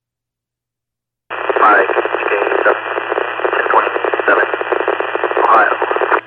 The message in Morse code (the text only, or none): none